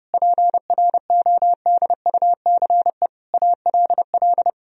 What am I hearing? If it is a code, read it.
Morse code, heard as a keyed tone, about 30 wpm, 700 Hz. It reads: PRODUCE ALL
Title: Sentences from Top 1000 Words - Review Part 3 - 30wpm